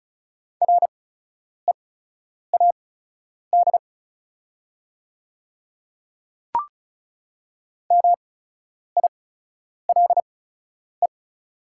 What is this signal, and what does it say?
Morse code, heard as a keyed tone, about 36 wpm, spaced as for 10 wpm, 690 Hz. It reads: READ MILE
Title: 8x Character Spacing - Top 300 Words - 35wpm